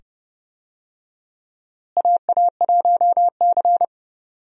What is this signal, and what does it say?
Morse code, heard as a keyed tone, about 31 wpm, 700 Hz. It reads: AA1C
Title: USA Call Signs - 4 Characters - Repeated 2x - 30wpm